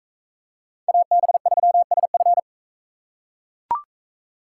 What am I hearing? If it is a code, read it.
Morse code, heard as a keyed tone, about 43 wpm, 700 Hz. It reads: AB3SF